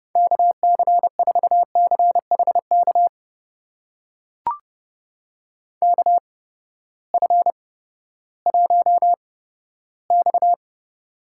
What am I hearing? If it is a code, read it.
Morse code, heard as a keyed tone, about 30 wpm, spaced as for 15 wpm, 700 Hz. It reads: KC4CHK KF1X
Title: USA Call Signs - 8x Character Spacing - 30wpm